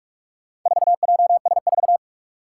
Morse code, heard as a keyed tone, about 46 wpm, 700 Hz. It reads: 4JS4
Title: Four Challenging Letter-Number Combinations 45wpm